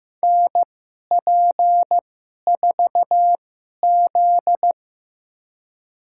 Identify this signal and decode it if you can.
Morse code, heard as a keyed tone, about 15 wpm, spaced as for 11 wpm, 700 Hz. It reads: NP4Z
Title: USA Call Signs - 4 Characters - 2x Character Spacing - 15wpm